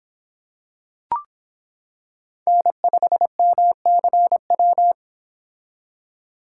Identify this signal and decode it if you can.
Morse code, heard as a keyed tone, about 26 wpm, 700 Hz. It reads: N5MCW